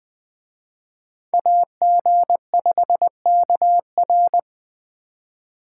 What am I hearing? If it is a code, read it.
Morse code, heard as a keyed tone, about 20 wpm, 700 Hz. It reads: AG5KR
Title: Speed Racing - USA Call Signs 17wpm